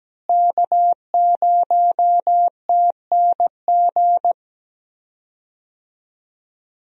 Morse code, heard as a keyed tone, about 17 wpm, 700 Hz. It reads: K0TNG
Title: USA Call Signs - 5 Characters - Repeated 2x - 17wpm